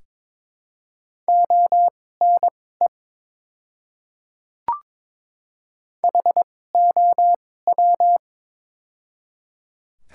Morse code, heard as a keyed tone, about 22 wpm, spaced as for 16 wpm, 700 Hz. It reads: ONE HOW
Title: 2x Character Spacing - Top 200 Words - 22wpm